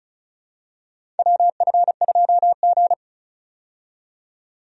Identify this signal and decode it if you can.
Morse code, heard as a keyed tone, about 36 wpm, 700 Hz. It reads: WF2G
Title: USA Call Signs - 4 Characters - 35wpm